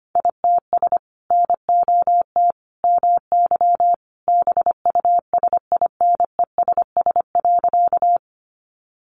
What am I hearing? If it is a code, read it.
Morse code, heard as a keyed tone, about 25 wpm, 700 Hz. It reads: ITS NOT MY BUSINESS.
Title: Sentences from Top 600 Words 25wpm